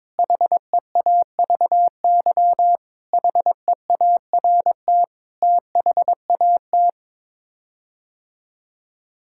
Morse code, heard as a keyed tone, about 22 wpm, 700 Hz. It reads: HEAVY HEART THAT
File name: practice / 22wpm / Sets of 7 Words - 22wpm